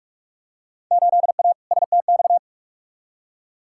Morse code, heard as a keyed tone, about 46 wpm, 700 Hz. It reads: 8A STX
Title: Field Day Exchanges Encore - 45wpm